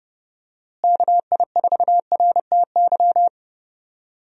Morse code, heard as a keyed tone, about 30 wpm, 700 Hz. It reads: KI4RTY
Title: USA Call Signs - 6 Characters - Repeated 2x - 30wpm